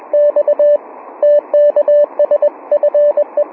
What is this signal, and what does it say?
Morse code, heard as a keyed tone, about 20 wpm, 590 Hz. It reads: X TKS FE